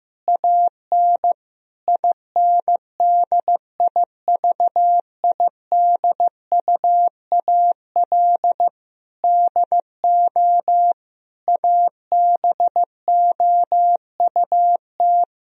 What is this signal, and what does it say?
Morse code, heard as a keyed tone, about 16 wpm, 700 Hz. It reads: AN INDIVIDUAL DO ABOUT
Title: Sentences from Top 1000 Words - Review Part 4 - 15wpm